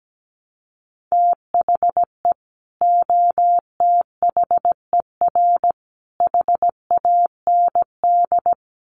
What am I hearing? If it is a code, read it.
Morse code, heard as a keyed tone, about 17 wpm, 700 Hz. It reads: THE OTHER HAND
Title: Sets of 3 Words 17wpm